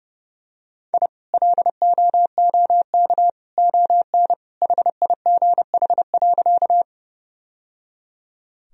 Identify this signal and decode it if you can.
Morse code, heard as a keyed tone, about 30 wpm, 700 Hz. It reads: I LOOK ON HIGH.
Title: Sentences from Top 1000 Words - Review Part 1 - 30wpm